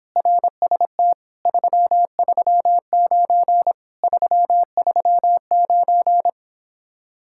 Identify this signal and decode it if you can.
Morse code, heard as a keyed tone, about 26 wpm, 700 Hz. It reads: RST 339 339